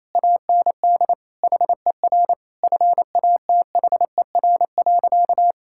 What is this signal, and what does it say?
Morse code, heard as a keyed tone, about 27 wpm, 700 Hz. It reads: AND HER FATHER.
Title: Sentences from Top 1000 Words - Review Part 6 - 28wpm